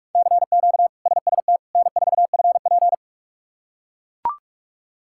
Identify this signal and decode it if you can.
Morse code, heard as a keyed tone, about 43 wpm, 700 Hz. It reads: CQ SST N4FP
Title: K1USN SST QSO Practice - 45wpm